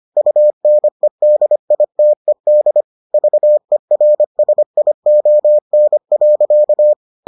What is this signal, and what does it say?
Morse code, heard as a keyed tone, about 25 wpm, 600 Hz. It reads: UNEDITED VERSION.